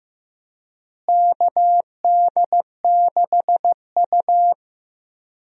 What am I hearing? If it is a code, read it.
Morse code, heard as a keyed tone, about 15 wpm, 700 Hz. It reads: KD6U
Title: USA Call Signs - 4 Characters - Repeated 3x - 15wpm